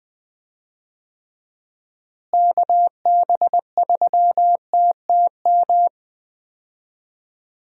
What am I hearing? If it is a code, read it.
Morse code, heard as a keyed tone, about 20 wpm, 700 Hz. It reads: KB3TTM
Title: USA Call Signs - 6 Characters - Repeated 3x - 20wpm